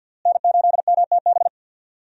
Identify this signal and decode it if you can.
Morse code, heard as a keyed tone, about 46 wpm, 700 Hz. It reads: N8MTB
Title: USA Call Signs - 5 Characters - Repeated 3x - 50wpm